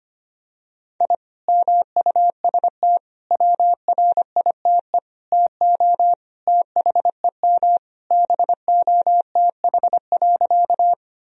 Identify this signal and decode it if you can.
Morse code, heard as a keyed tone, about 25 wpm, 700 Hz. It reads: I MUST WRITE TO THEM BOTH.
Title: Sentences from Top 1000 Words - Review Part 1 - 25wpm